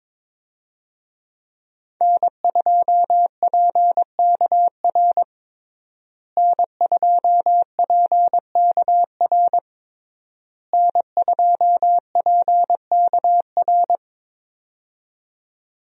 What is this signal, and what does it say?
Morse code, heard as a keyed tone, about 22 wpm, 700 Hz. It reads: N2PKR N2PKR N2PKR
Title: USA Call Signs - 5 Characters - Repeated 3x - 22wpm